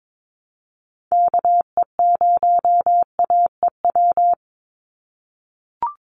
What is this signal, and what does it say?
Morse code, heard as a keyed tone, about 22 wpm, 700 Hz. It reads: KE0AEW
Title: USA Call Signs 22wpm